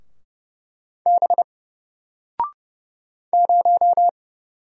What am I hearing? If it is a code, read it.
Morse code, heard as a keyed tone, about 30 wpm, 700 Hz. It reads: B 0
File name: Single-Letter-Number-30wpm